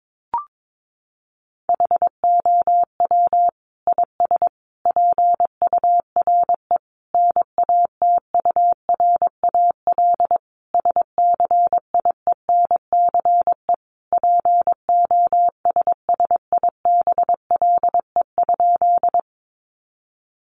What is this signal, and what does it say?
Morse code, heard as a keyed tone, about 22 wpm, 700 Hz. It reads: HOW IS PURE NATURAL SCIENCE POSSIBLE?
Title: Sentences from Top 1500 Words 22wpm